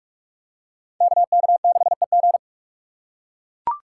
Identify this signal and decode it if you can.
Morse code, heard as a keyed tone, about 44 wpm, 700 Hz. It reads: KK6EG